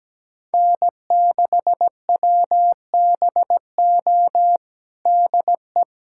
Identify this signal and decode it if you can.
Morse code, heard as a keyed tone, about 17 wpm, 700 Hz. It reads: N6WBO DE
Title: USA Callsign DE Callsign - 5 Characters - 17wpm